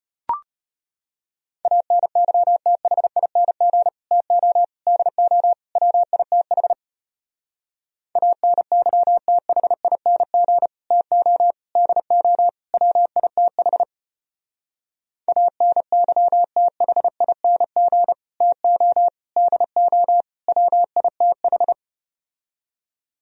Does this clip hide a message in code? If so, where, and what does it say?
Yes: Morse code from 1.7 to 21.7 s, heard as a keyed tone, about 33 wpm, 700 Hz. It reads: ANYTHING TO DO WITH ANYTHING TO DO WITH ANYTHING TO DO WITH